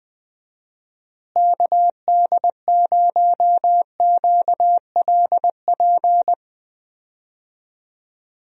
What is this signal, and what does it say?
Morse code, heard as a keyed tone, about 21 wpm, 700 Hz. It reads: KD0QLP
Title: USA Call Signs - 6 Characters - Repeated 3x - 20wpm